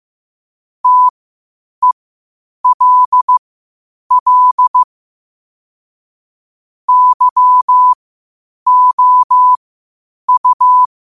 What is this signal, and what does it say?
Morse code, heard as a keyed tone, about 15 wpm, spaced as for 8 wpm, 1000 Hz. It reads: TELL YOU
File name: qotdfiles / 8wpm